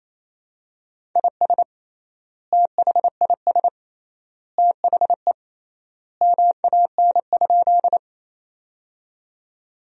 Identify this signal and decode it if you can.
Morse code, heard as a keyed tone, about 27 wpm, 700 Hz. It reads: IS THIS THE MAN?
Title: 3x Word-Spacing - Sentences from Top 200 Words - 28wpm